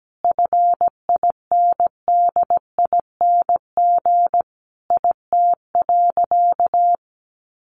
Morse code, heard as a keyed tone, about 17 wpm, 700 Hz. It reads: FINDING IT.
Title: Sentences from Top 1100 Words 17wpm